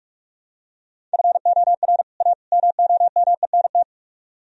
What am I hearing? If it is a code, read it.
Morse code, heard as a keyed tone, about 46 wpm, 700 Hz. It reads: FOR A MOMENT